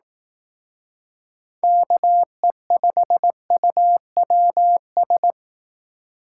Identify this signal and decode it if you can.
Morse code, heard as a keyed tone, about 18 wpm, 700 Hz. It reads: KE5UWS
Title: Speed Racing - USA 2x3 Call Signs 15wpm